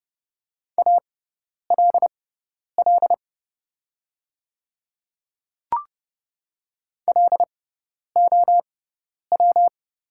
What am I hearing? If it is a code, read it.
Morse code, heard as a keyed tone, about 29 wpm, spaced as for 10 wpm, 700 Hz. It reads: ALL LOW
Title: 6x Character Spacing - Top 300 Words - 30wpm